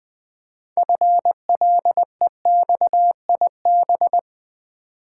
Morse code, heard as a keyed tone, about 20 wpm, 700 Hz. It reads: FLEXIB
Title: Build A Word - 3 Letters At A Time - 8-Letter Words - 20wpm